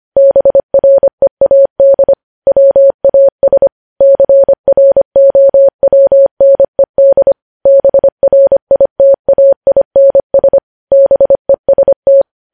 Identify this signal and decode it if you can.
Morse code, heard as a keyed tone, about 25 wpm, 550 Hz. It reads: BREAD WAS CROWNED BRITAINS BEST
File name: podcast / News headlines-1x-25x25